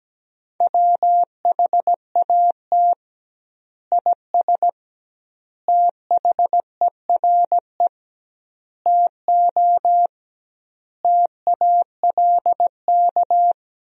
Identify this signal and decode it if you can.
Morse code, heard as a keyed tone, about 17 wpm, 700 Hz. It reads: WHAT IS THERE TO TALK
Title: 2x Word-Spacing - Sentences from Top 400 Words - 17wpm